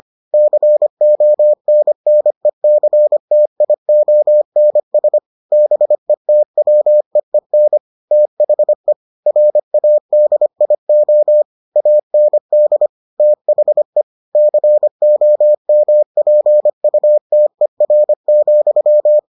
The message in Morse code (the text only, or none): CONNECTIONS BETWEEN THE RADIO AND THE COMPUTER,